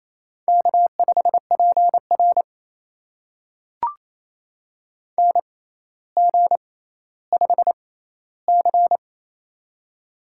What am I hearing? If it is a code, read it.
Morse code, heard as a keyed tone, about 28 wpm, spaced as for 14 wpm, 700 Hz. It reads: K5PR NG5C
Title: USA Call Signs - 4 Characters - 6x Character Spacing - 28wpm